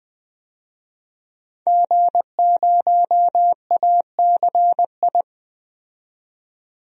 Morse code, heard as a keyed tone, about 20 wpm, 700 Hz. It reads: G0ACI
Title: DXCC-100 Call Signs - 5 Characters - 20wpm